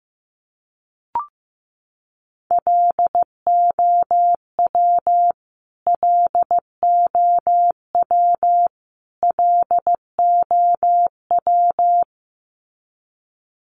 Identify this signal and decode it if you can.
Morse code, heard as a keyed tone, about 15 wpm, 700 Hz. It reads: LOW LOW LOW
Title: Top 300 Words Repeated 15wpm